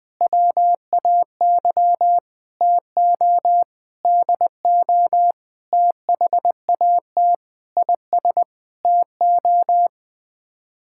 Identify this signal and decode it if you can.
Morse code, heard as a keyed tone, about 20 wpm, 700 Hz. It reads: WAY TO DO THAT IS TO
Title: Sets of 8 Words - 20wpm